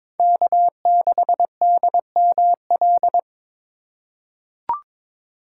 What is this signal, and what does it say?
Morse code, heard as a keyed tone, about 22 wpm, 700 Hz. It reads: K6DML
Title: USA Call Signs - 5 Characters - 3x Character Spacing - 22wpm